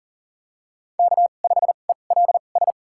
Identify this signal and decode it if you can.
Morse code, heard as a keyed tone, about 40 wpm, spaced as for 29 wpm, 700 Hz. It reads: K5ELS
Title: USA Call Signs - 5 Characters - 2x Character Spacing - 40wpm